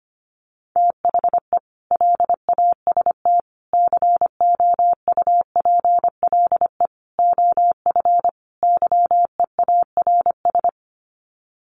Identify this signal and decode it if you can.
Morse code, heard as a keyed tone, about 25 wpm, 700 Hz. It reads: THE LAST COUPLE OF YEARS
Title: Sets of 5 Words 25wpm